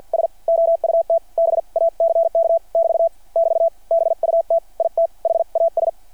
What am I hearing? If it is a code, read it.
Morse code, heard as a keyed tone, about 51 wpm, spaced as for 43 wpm, 660 Hz. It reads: S OUT BACK <BT> <BT> BUT IT HAS
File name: archived / CRSnet